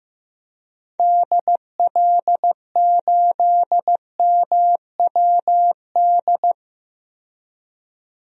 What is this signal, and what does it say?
Morse code, heard as a keyed tone, about 15 wpm, 700 Hz. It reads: DL8MWD
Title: DXCC-100 Call Signs - 6 Characters - 15wpm